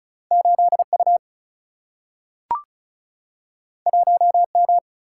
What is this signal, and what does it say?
Morse code, heard as a keyed tone, about 36 wpm, 700 Hz. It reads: 8U 1M
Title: Single-Letter Single-Number 35wpm